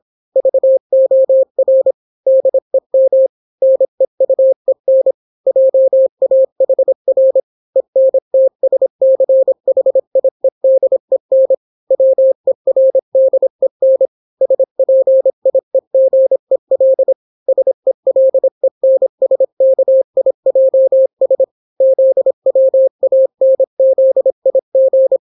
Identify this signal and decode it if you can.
Morse code, heard as a keyed tone, about 26 wpm, 530 Hz. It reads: VOR DEM NEUEN JAHR ENTSCHIEDEN WERDEN SPIEGEL SELENSKIJS ZWANZIG